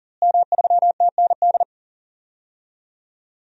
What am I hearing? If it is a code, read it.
Morse code, heard as a keyed tone, about 40 wpm, 700 Hz. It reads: M3TND
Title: DXCC-100 Call Signs - 5 Characters - Repeated 2x - 40wpm